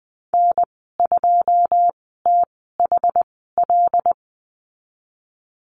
Morse code, heard as a keyed tone, about 20 wpm, spaced as for 15 wpm, 700 Hz. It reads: N2THL
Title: 088 Lesson 15g - Number 5 - Call Signs 20wpm